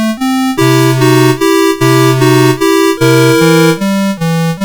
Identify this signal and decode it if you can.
There is no code.